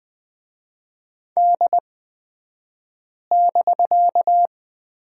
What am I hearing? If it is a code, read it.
Morse code, heard as a keyed tone, about 20 wpm, spaced as for 5 wpm, 700 Hz. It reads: D<BK>
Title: 249 Lesson 40d - BK - Two Characters 20wpm